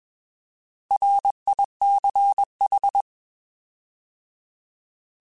Morse code, heard as a keyed tone, about 21 wpm, 800 Hz. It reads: RICH